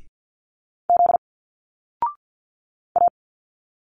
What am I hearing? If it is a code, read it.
Morse code, heard as a keyed tone, about 50 wpm, 700 Hz. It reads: Z A
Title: Single Letters 50wpm